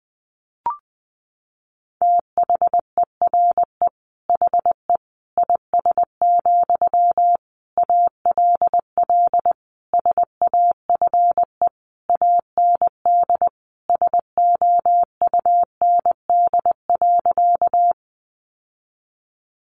Morse code, heard as a keyed tone, about 19 wpm, 700 Hz. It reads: THERE HE IS, ALL SAFE AND SOUND.
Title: Sentences from Top 900 Words 20wpm